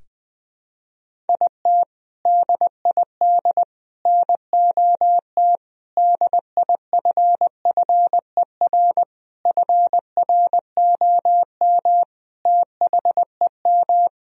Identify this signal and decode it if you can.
Morse code, heard as a keyed tone, about 20 wpm, 700 Hz. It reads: IT DID NOT DIFFER FROM THEM
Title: Sentences from Top 1000 Words - Review Part 1 - 20wpm